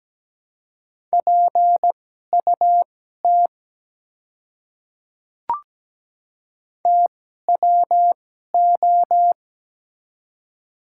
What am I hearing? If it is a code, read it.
Morse code, heard as a keyed tone, about 17 wpm, spaced as for 12 wpm, 700 Hz. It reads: PUT TWO
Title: 2x Character Spacing - Top 300 Words - 17wpm